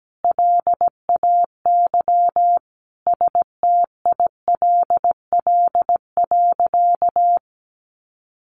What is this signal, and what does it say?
Morse code, heard as a keyed tone, about 17 wpm, 700 Hz. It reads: LAY STILL.
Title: Sentences from Top 1000 Words 17wpm